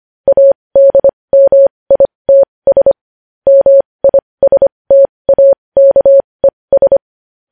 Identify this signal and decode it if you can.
Morse code, heard as a keyed tone, about 25 wpm, spaced as for 20 wpm, 550 Hz. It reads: ADMITS MISTAKES
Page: news headlines in morse code at 20 wpm